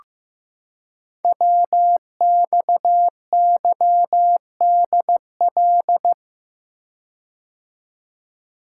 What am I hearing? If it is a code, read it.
Morse code, heard as a keyed tone, about 15 wpm, 700 Hz. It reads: WXYDL
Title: Code Groups - 5 Letters - 15wpm